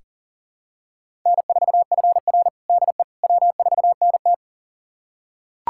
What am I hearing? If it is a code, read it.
Morse code, heard as a keyed tone, about 40 wpm, 700 Hz. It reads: N4FR DE W4NT
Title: USA Callsign DE Callsign - 4 Characters - 40wpm